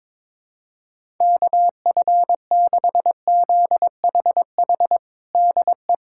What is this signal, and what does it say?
Morse code, heard as a keyed tone, about 22 wpm, 700 Hz. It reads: KF6ZHH DE
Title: USA Callsign DE Callsign 22wpm